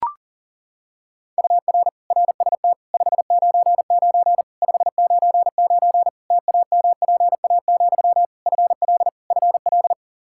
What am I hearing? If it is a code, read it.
Morse code, heard as a keyed tone, about 40 wpm, 700 Hz. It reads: UR RST 599 599 TAMPA, FL FL
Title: K1USN SST QSO Practice 40wpm